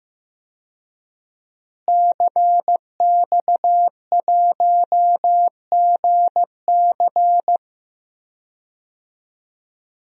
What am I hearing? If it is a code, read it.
Morse code, heard as a keyed tone, about 15 wpm, 700 Hz. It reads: CX1GC
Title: DXCC-100 Call Signs - 5 Characters - Repeated 3x - 15wpm